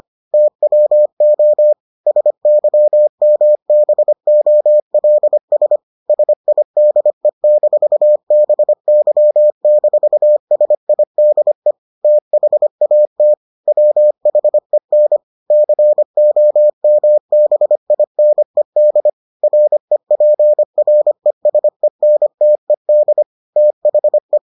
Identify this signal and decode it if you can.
Morse code, heard as a keyed tone, about 25 wpm, 600 Hz. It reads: TWO SYMBOLS SIDE-BY-SIDE THAT WHEN COMBINED REPRESENTED THE